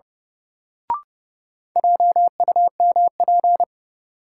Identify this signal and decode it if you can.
Morse code, heard as a keyed tone, about 31 wpm, 700 Hz. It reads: JUMP
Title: Sentences from Top 1300 Words - New Word Drill - 30wpm